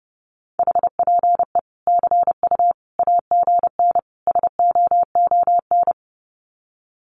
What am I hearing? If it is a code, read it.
Morse code, heard as a keyed tone, about 30 wpm, 700 Hz. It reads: HPE CU AGN SOON
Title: Common QSO Phrases 30wpm